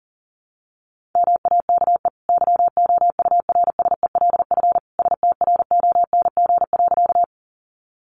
Morse code, heard as a keyed tone, about 40 wpm, 700 Hz. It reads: MAKE YOURSELF STRONG.
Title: Sentences from Top 600 Words 40wpm